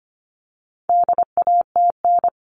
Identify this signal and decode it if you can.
Morse code, heard as a keyed tone, about 25 wpm, 700 Hz. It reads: DATN
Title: All Cut Numbers - 4-Digit 25wpm